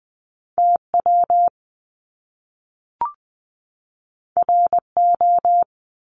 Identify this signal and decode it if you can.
Morse code, heard as a keyed tone, about 20 wpm, 700 Hz. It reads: T W R O
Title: Top Two Letter Combinations in Words 20wpm